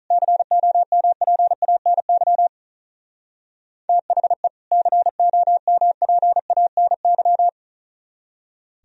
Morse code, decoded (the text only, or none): COMPANY THE COMPANY